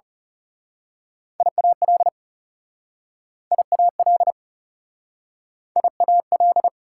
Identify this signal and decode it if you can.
Morse code, heard as a keyed tone, about 37 wpm, 700 Hz. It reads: IAL IAL IAL
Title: Speed-Racing Prefixes and Suffixes 30wpm